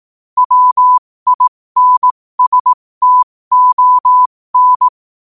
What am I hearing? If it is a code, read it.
Morse code, heard as a keyed tone, about 18 wpm, spaced as for 15 wpm, 1000 Hz. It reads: WINSTON
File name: qotdfiles / 15wpm